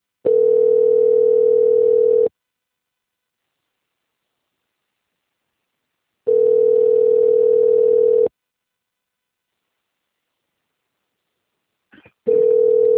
Ringback tone (North America), 440 + 480 Hz.